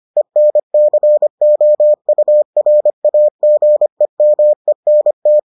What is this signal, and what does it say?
Morse code, heard as a keyed tone, about 25 wpm, 600 Hz. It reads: ENCOURAGEMENT